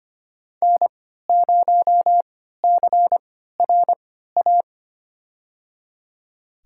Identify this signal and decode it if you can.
Morse code, heard as a keyed tone, about 25 wpm, spaced as for 14 wpm, 700 Hz. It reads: N0CRA
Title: USA Call Signs - 5 Characters - 3x Character Spacing - 25wpm